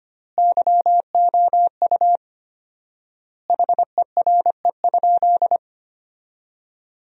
Morse code, heard as a keyed tone, about 25 wpm, 700 Hz. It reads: YOU HERE?
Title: 4x Word-Spacing - Sentences from Top 200 Words - 25wpm